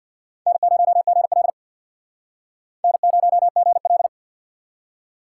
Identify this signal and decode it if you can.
Morse code, heard as a keyed tone, about 65 wpm, 690 Hz. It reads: N0GL N0GL